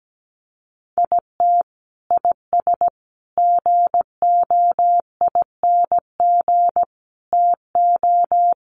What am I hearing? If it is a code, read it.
Morse code, heard as a keyed tone, about 17 wpm, 700 Hz. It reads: IT IS GOING TO
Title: Sets of 5 Words 17wpm